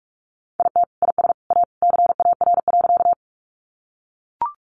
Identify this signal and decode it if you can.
Morse code, heard as a keyed tone, about 44 wpm, 700 Hz. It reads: IT IS A CAR.
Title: Easy Phrases 45wpm